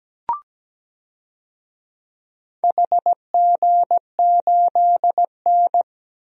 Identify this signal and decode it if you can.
Morse code, heard as a keyed tone, about 17 wpm, 700 Hz. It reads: HG8N
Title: DXCC-100 Call Signs - 4 Characters - Repeated 2x - 17wpm